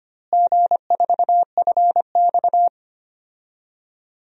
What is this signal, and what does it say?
Morse code, heard as a keyed tone, about 25 wpm, 700 Hz. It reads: G4FX
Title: DXCC-100 Call Signs - 4 Characters - Repeated 3x - 25wpm